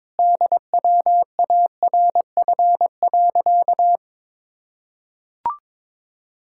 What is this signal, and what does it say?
Morse code, heard as a keyed tone, about 22 wpm, 700 Hz. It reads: DWARF.
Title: Pangram Challenge v2 - 22wpm